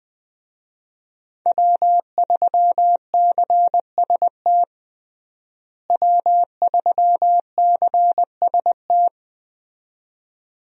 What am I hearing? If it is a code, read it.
Morse code, heard as a keyed tone, about 21 wpm, 700 Hz. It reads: W3CST W3CST
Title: USA Call Signs - 5 Characters - Repeated 2x - 20wpm